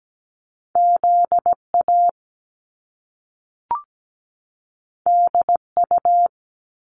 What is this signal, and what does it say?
Morse code, heard as a keyed tone, about 17 wpm, 700 Hz. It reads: ZA DU
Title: Top Two Letter Combinations in Words 17wpm